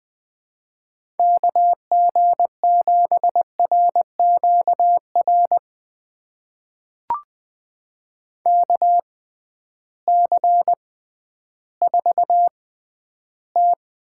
Morse code, heard as a keyed tone, about 20 wpm, spaced as for 12 wpm, 700 Hz. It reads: KG7RQR KC4T
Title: USA Call Signs - 6 Characters - 6x Character Spacing - 20wpm